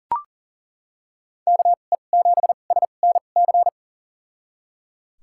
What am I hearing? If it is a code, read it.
Morse code, heard as a keyed tone, about 37 wpm, spaced as for 28 wpm, 700 Hz. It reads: KE7SNC
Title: USA Call Signs - 6 Characters - 2x Character Spacing - 40wpm